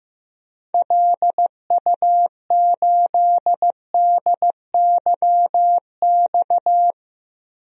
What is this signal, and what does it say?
Morse code, heard as a keyed tone, about 16 wpm, 700 Hz. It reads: LU8DYX
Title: DXCC-100 Call Signs - 6 Characters - Repeated 3x - 15wpm